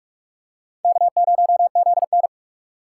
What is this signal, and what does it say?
Morse code, heard as a keyed tone, about 40 wpm, 700 Hz. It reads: K0ZN